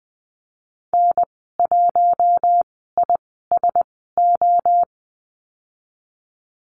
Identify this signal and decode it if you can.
Morse code, heard as a keyed tone, about 20 wpm, spaced as for 14 wpm, 700 Hz. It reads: N1ISO
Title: 062 Lesson 11g - Letter h - Call Signs 20wpm